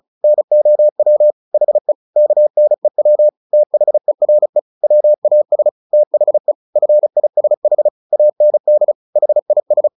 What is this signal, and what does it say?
Morse code, heard as a keyed tone, about 34 wpm, 600 Hz. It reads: NOW HE KNEW THERE WAS THE FISH AND HIS
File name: the old man and the sea 2 35wpm